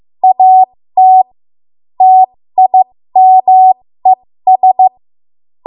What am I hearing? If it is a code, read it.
Morse code, heard as a keyed tone, about 15 wpm, spaced as for 13 wpm, 750 Hz. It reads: AT TIMES
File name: Morse / 13WPM